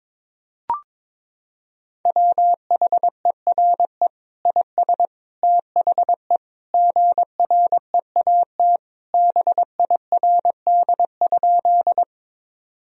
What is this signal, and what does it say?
Morse code, heard as a keyed tone, about 22 wpm, 700 Hz. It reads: WHERE IS THE GREAT BIRD?
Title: Sentences from Top 1000 Words - Review Part 2 - 22wpm